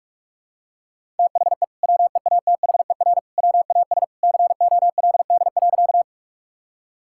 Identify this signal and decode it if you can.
Morse code, heard as a keyed tone, about 41 wpm, 690 Hz. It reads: THE WEATHER WAS COLD.
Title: Sentences from Top 1000 Words - Review Part 5 - 45wpm